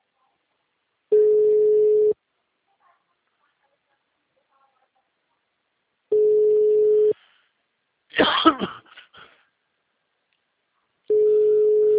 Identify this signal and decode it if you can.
Ringback tone (Europe, 425 Hz).